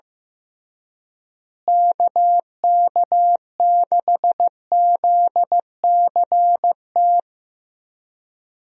Morse code, heard as a keyed tone, about 15 wpm, 700 Hz. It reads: KK6ZCT
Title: USA Call Signs - 6 Characters - Repeated 3x - 15wpm